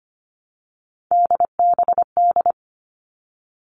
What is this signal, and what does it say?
Morse code, heard as a keyed tone, about 25 wpm, 700 Hz. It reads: DBD
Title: All Cut Numbers - 3-Digit 25wpm